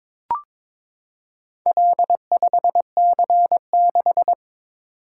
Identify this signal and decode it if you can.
Morse code, heard as a keyed tone, about 22 wpm, 700 Hz. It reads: L5C6
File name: practice / 22wpm / Four Letter-Number Combinations - 22wpm